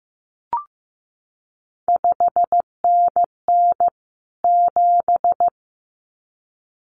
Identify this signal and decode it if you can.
Morse code, heard as a keyed tone, about 15 wpm, 700 Hz. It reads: 5NN 7
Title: CQ WW DX Contest Exchange Encore - 15wpm